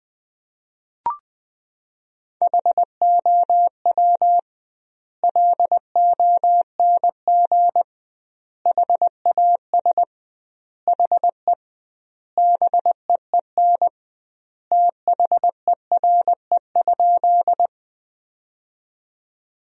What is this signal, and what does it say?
Morse code, heard as a keyed tone, about 20 wpm, 700 Hz. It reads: HOW LONG HAS HE BEEN THERE?
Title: 2x Word-Spacing - Sentences from Top 100 Words - 20wpm